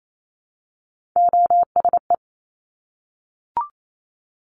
Morse code, heard as a keyed tone, about 28 wpm, 700 Hz. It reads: OSE
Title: Top Three Letter Combinations in Words 28wpm